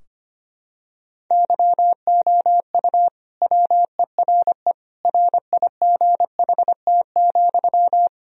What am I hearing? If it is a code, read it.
Morse code, heard as a keyed tone, about 25 wpm, 700 Hz. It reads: YOU WERE RIGHT,
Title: Sentences from Top 1000 Words - Review Part 2 - 25wpm